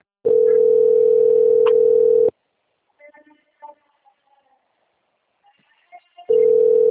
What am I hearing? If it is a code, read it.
Ringback tone (North America), 440 + 480 Hz.
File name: out- - - - - .11799